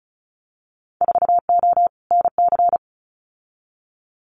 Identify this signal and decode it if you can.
Morse code, heard as a keyed tone, about 36 wpm, 690 Hz. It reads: 4O NC
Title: Winter Field Day Exchanges 35wpm